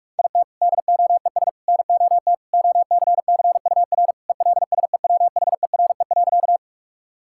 Morse code, heard as a keyed tone, about 46 wpm, 700 Hz. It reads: IT DOES NOT OCCUR ELSEWHERE.
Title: Sentences from Top 1600 Words 45wpm